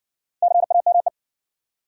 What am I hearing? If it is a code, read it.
Morse code, heard as a keyed tone, about 60 wpm, 700 Hz. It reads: CAME